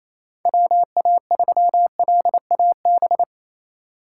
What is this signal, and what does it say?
Morse code, heard as a keyed tone, about 28 wpm, 700 Hz. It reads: WA3LAB
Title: K1USN SST QSO Practice - 28wpm